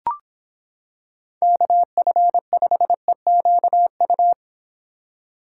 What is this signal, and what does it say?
Morse code, heard as a keyed tone, about 26 wpm, 700 Hz. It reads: KF5EQU